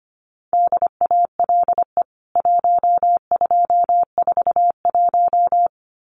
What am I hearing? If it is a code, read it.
Morse code, heard as a keyed tone, about 25 wpm, 700 Hz. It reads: DALE 1241
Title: CWT Exchange 25wpm